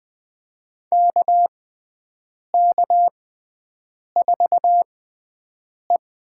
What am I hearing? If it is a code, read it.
Morse code, heard as a keyed tone, about 20 wpm, spaced as for 7 wpm, 700 Hz. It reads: KK4E
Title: USA Call Signs - 6 Characters - 6x Character Spacing - 20wpm